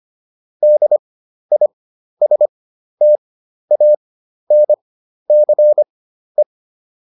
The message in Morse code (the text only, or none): DISTANCE